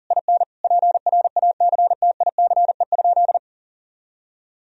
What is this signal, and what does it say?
Morse code, heard as a keyed tone, about 40 wpm, 700 Hz. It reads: IN PRACTICE?
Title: Sentences from Top 1000 Words - Review Part 4 - 40wpm